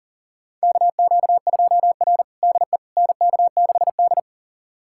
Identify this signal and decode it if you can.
Morse code, heard as a keyed tone, about 37 wpm, 700 Hz. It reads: KQ2R DE NK6D
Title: USA Callsign DE Callsign - 4 Characters - 40wpm